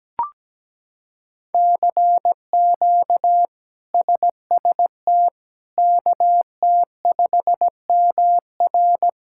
Morse code, heard as a keyed tone, about 17 wpm, 700 Hz. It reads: CQ SST KT5MR
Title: K1USN SST QSO Practice - 17wpm